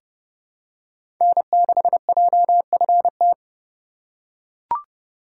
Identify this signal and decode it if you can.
Morse code, heard as a keyed tone, about 30 wpm, 700 Hz. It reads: N6JFT
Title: USA Call Signs - 5 Characters - 3x Character Spacing - 30wpm